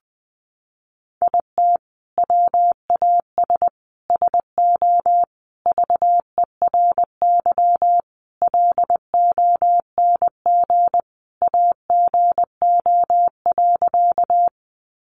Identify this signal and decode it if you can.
Morse code, heard as a keyed tone, about 20 wpm, 700 Hz. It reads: IT WAS SO VERY LONG AGO.
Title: Sentences from Top 500 Words 20wpm